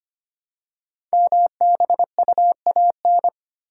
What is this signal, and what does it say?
Morse code, heard as a keyed tone, about 25 wpm, 700 Hz. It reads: MBUAN
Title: Code Groups - 5 Letters - 25wpm